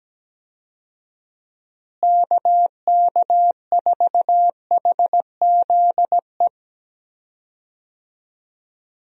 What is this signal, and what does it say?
Morse code, heard as a keyed tone, about 17 wpm, 700 Hz. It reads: KK4HZE
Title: USA Call Signs - 6 Characters - Repeated 2x - 17wpm